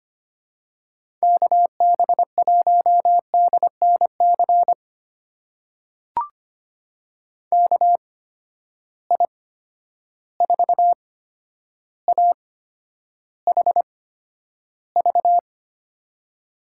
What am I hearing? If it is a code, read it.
Morse code, heard as a keyed tone, about 25 wpm, spaced as for 11 wpm, 700 Hz. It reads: KB1DNC KI4AHV
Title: USA Call Signs - 6 Characters - 8x Character Spacing - 25wpm